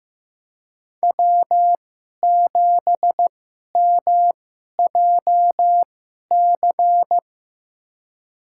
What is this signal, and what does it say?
Morse code, heard as a keyed tone, about 15 wpm, spaced as for 11 wpm, 700 Hz. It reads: W7MJC
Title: USA Call Signs - 2x Character Spacing - 15wpm